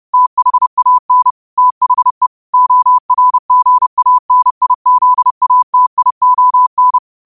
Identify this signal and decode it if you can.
Morse code, heard as a keyed tone, about 30 wpm, 1000 Hz. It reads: THAN THE ORGANIZATION
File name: qotdfiles / 30wpm